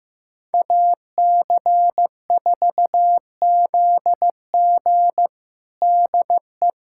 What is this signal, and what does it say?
Morse code, heard as a keyed tone, about 15 wpm, 700 Hz. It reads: AC4ZG DE
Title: USA Callsign DE Callsign - 5 Characters - 15wpm